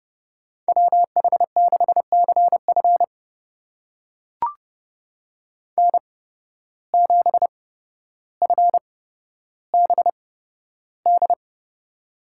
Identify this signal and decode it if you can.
Morse code, heard as a keyed tone, about 31 wpm, spaced as for 13 wpm, 700 Hz. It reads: WH6CF N7FBD
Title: USA Call Signs - 8x Character Spacing - 30wpm